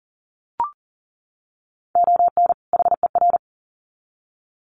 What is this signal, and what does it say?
Morse code, heard as a keyed tone, about 40 wpm, 700 Hz. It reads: ON HER
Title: Sets of 2 Words 40wpm